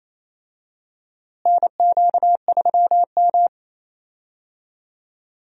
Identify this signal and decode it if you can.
Morse code, heard as a keyed tone, about 28 wpm, 700 Hz. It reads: NQ3M
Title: USA Call Signs - 4 Characters - Repeated 2x - 28wpm